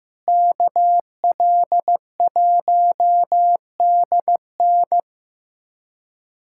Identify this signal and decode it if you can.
Morse code, heard as a keyed tone, about 15 wpm, 700 Hz. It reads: KL1DN
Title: USA Callsign DE Callsign - 5 Characters - 15wpm